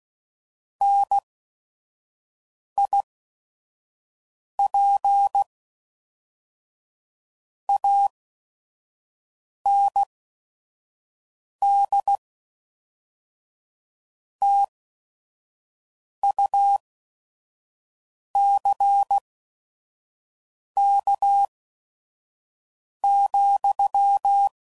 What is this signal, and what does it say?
Morse code, heard as a keyed tone, about 16 wpm, spaced as for 5 wpm, 800 Hz. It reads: NIP AND TUCK,